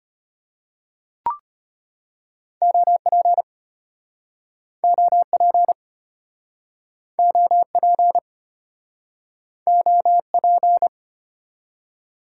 Morse code, heard as a keyed tone, about 34 wpm, 700 Hz. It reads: OP OP OP OP